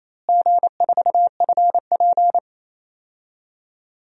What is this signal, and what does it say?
Morse code, heard as a keyed tone, about 28 wpm, 700 Hz. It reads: G4FP
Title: DXCC-100 Call Signs - 4 Characters - Repeated 2x - 28wpm